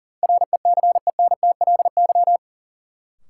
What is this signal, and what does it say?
Morse code, heard as a keyed tone, about 38 wpm, 700 Hz. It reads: RECENTLY
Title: Sentences from Top 1200 Words - New Word Drill Alt - 40wpm